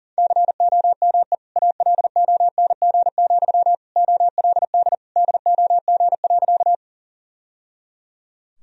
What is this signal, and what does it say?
Morse code, heard as a keyed tone, about 40 wpm, 700 Hz. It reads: COME ALONG, OLD DOG.
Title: Sentences from Top 1000 Words - Review Part 2 - 40wpm